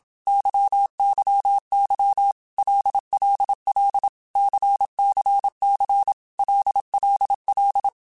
Morse code, heard as a keyed tone, about 27 wpm, 800 Hz. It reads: YYY LLL CCC LLL
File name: Basic Letter Groups By7Groups / LFCYQ / LFCYQ@25